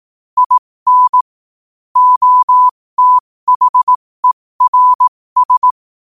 Morse code, heard as a keyed tone, about 18 wpm, spaced as for 15 wpm, 1000 Hz. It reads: IN OTHERS
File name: qotdfiles / 15wpm